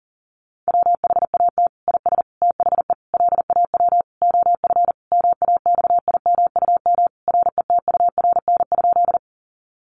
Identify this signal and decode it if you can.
Morse code, heard as a keyed tone, about 37 wpm, 700 Hz. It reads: WHAT IS THE LAW OF MAXIMUM RETURN?
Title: Sentences from Top 1300 Words 40wpm